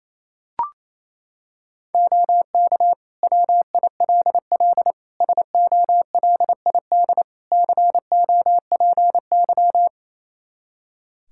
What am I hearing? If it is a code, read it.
Morse code, heard as a keyed tone, about 28 wpm, 700 Hz. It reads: OK WILL SOLID COPY
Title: Practice QSO - Exchange 3 - More Info - 28wpm